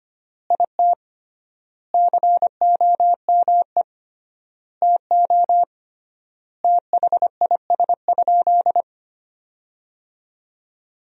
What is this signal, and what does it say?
Morse code, heard as a keyed tone, about 25 wpm, 700 Hz. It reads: IT COME TO THIS?
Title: 3x Word-Spacing - Sentences from Top 100 Words - 25wpm